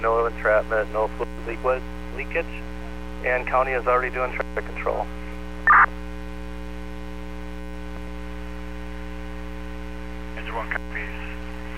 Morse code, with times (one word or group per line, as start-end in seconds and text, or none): none